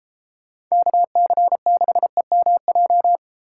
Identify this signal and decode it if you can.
Morse code, heard as a keyed tone, about 34 wpm, 700 Hz. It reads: KC6EMJ
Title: Speed Racing - USA 2x3 Call Signs 28wpm